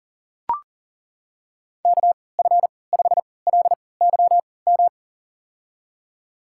Morse code, heard as a keyed tone, about 38 wpm, spaced as for 22 wpm, 700 Hz. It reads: KF5LYM
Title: USA Call Signs - 6 Characters - 3x Character Spacing - 40wpm